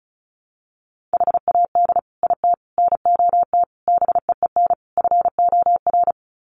Morse code, heard as a keyed tone, about 34 wpm, 700 Hz. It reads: HAD IT NOT BEEN FOR
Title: Sets of 5 Words Encore2 35wpm